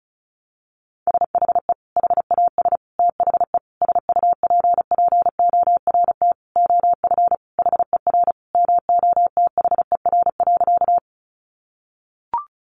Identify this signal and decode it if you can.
Morse code, heard as a keyed tone, about 34 wpm, 700 Hz. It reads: SHE HAS THE SUPPORT OF HER MOTHER.
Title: Sentences from Top 800 Words 35wpm